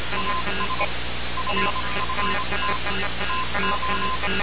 Morse code, heard as a keyed tone, about 26 wpm, spaced as for 22 wpm, 1030 Hz, with noise. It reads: AS FFT TO
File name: radiocap / REC z